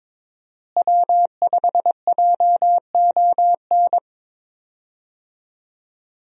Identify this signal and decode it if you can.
Morse code, heard as a keyed tone, about 22 wpm, 700 Hz. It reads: W5JON
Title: USA Call Signs - 5 Characters - Repeated 2x - 22wpm